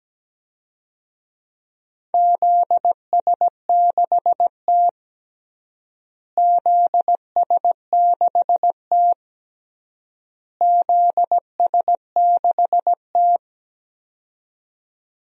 Morse code, heard as a keyed tone, about 17 wpm, 700 Hz. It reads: ZS6T ZS6T ZS6T
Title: DXCC-100 Call Signs - 4 Characters - Repeated 3x - 17wpm